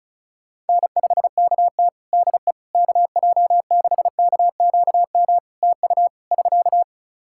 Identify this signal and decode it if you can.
Morse code, heard as a keyed tone, about 36 wpm, 700 Hz. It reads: N5KT DE KJ6KQM TU <SK>
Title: Practice QSO - Exchange 4 - Ending - 35wpm